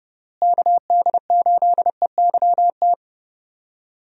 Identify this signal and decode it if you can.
Morse code, heard as a keyed tone, about 32 wpm, 700 Hz. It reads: KD8EYT
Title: USA Call Signs - 6 Characters - Repeated 3x - 30wpm